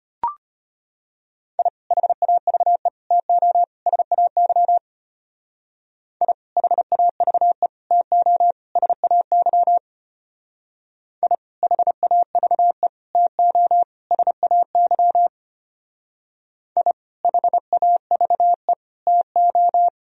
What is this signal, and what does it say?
Morse code, heard as a keyed tone, about 30 wpm, 700 Hz. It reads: I HAVE TO SAY I HAVE TO SAY I HAVE TO SAY I HAVE TO
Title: Speed-Racing - Sets of 4 Words 25wpm